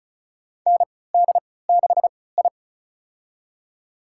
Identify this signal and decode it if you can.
Morse code, heard as a keyed tone, about 36 wpm, spaced as for 20 wpm, 700 Hz. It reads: ND6I
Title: USA Call Signs - 4 Characters - 3x Character Spacing - 35wpm